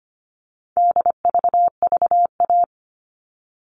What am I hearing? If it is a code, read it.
Morse code, heard as a keyed tone, about 25 wpm, 700 Hz. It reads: DVVA
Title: All Cut Numbers - 4-Digit 25wpm